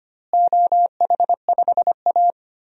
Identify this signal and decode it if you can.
Morse code, heard as a keyed tone, about 25 wpm, 700 Hz. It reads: OH5A
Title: DXCC-100 Call Signs - 4 Characters - Repeated 3x - 25wpm